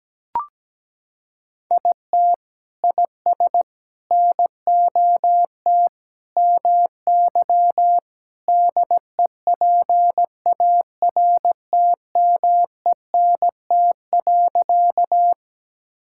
Morse code, heard as a keyed tone, about 17 wpm, 700 Hz. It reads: IT IS NOT MY DEPARTMENT.
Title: Sentences from Top 1000 Words - Review Part 5 - 17wpm